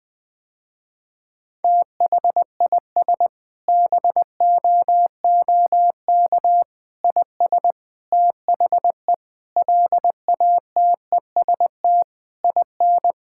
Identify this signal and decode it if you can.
Morse code, heard as a keyed tone, about 20 wpm, 700 Hz. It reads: THIS BOOK IS THE LATEST IN
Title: Sentences with 7-Word Phrases - Spoken Beginning - 20wpm